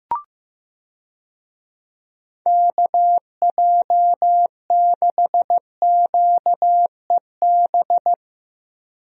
Morse code, heard as a keyed tone, about 15 wpm, 700 Hz. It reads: KJ6QEB
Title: USA Call Signs - 6 Characters - Repeated 2x - 15wpm